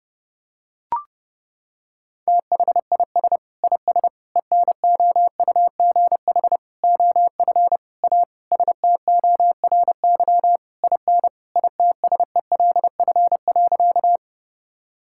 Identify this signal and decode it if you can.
Morse code, heard as a keyed tone, about 30 wpm, 700 Hz. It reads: THIS IS ENOUGH OF A STORY IN ITSELF.
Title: Sentences from Top 1000 Words - Review Part 5 - 30wpm